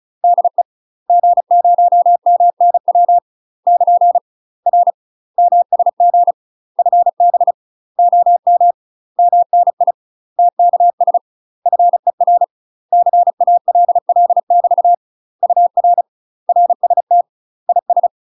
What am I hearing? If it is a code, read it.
Morse code, heard as a keyed tone, about 33 wpm, 700 Hz. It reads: DE G0MNW <KN> R MSG FB OM MNI TKS FER CALL <BT> UR RST IS
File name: UK MASTER QSOs TRAINING / 35WPM / UK MASTER QSOs 0036